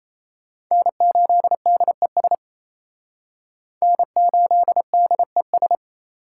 Morse code, heard as a keyed tone, about 34 wpm, 700 Hz. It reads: N8DES N8DES
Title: Speed Racing - USA 1x3 Call Signs 28wpm